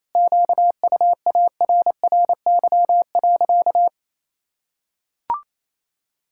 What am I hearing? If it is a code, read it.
Morse code, heard as a keyed tone, about 28 wpm, 700 Hz. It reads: QUARRY.
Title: Pangram Challenge v2 - 28wpm